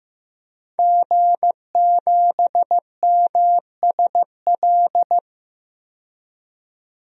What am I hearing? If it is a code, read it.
Morse code, heard as a keyed tone, about 15 wpm, 700 Hz. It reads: G7MSL
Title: DXCC-100 Call Signs - 5 Characters - 15wpm